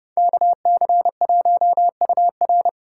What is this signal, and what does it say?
Morse code, heard as a keyed tone, about 30 wpm, 700 Hz. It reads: KC1UR